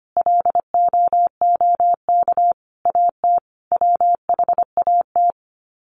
Morse code, heard as a keyed tone, about 25 wpm, 700 Hz. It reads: LOOK AT WHAT
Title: Sets of 5 Words Encore2 25wpm